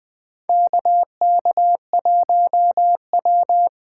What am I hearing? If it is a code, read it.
Morse code, heard as a keyed tone, about 20 wpm, 700 Hz. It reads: KK1W
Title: 2023 SOTA Chaser Call Signs - 20wpm